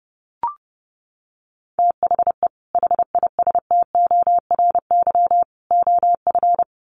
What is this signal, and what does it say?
Morse code, heard as a keyed tone, about 30 wpm, 700 Hz. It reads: THE HISTORY OF
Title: Sets of 3 Words 30wpm